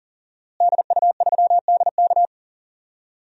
Morse code, heard as a keyed tone, about 40 wpm, 700 Hz. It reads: DU3DK
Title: DXCC-100 Call Signs - 5 Characters - Repeated 3x - 40wpm